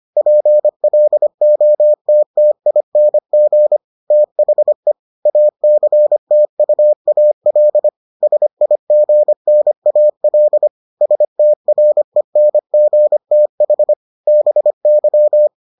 Morse code, heard as a keyed tone, about 25 wpm, 600 Hz. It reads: PLOTTING THE ACTUAL SIGNAL STRENGTH BY